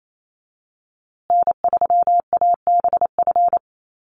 Morse code, heard as a keyed tone, about 28 wpm, 700 Hz. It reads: N3ABF
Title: USA Call Signs Repeated 2x 28wpm